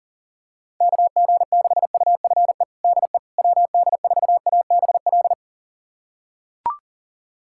Morse code, heard as a keyed tone, about 37 wpm, 700 Hz. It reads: KG6UFE DE WD4ABL
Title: USA Callsign DE Callsign 40wpm